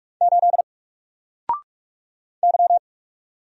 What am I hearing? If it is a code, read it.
Morse code, heard as a keyed tone, about 44 wpm, 700 Hz. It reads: 8 Y
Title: Single Letter-Number Alt 45wpm